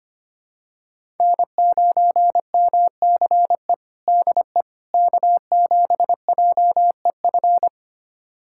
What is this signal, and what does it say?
Morse code, heard as a keyed tone, about 25 wpm, 700 Hz. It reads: N9MCE DE K7JEF
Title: USA Callsign DE Callsign - 5 Characters - 25wpm